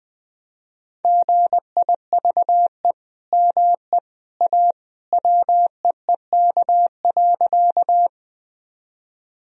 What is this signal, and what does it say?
Morse code, heard as a keyed tone, about 20 wpm, 700 Hz. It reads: GIVE ME A WEEK.
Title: Sentences from Top 1000 Words - Review Part 2 - 20wpm